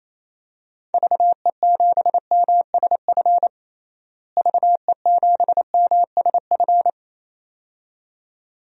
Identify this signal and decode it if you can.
Morse code, heard as a keyed tone, about 28 wpm, 700 Hz. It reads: VE7MSF VE7MSF
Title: DXCC-100 Call Signs - 6 Characters - Repeated 2x - 28wpm